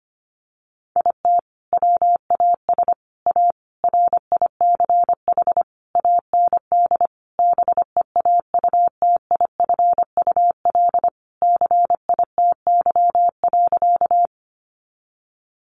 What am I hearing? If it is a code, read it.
Morse code, heard as a keyed tone, about 25 wpm, 700 Hz. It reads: IT WAS A RICH AND BEAUTIFUL CITY.
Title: Sentences from Top 1100 Words 25wpm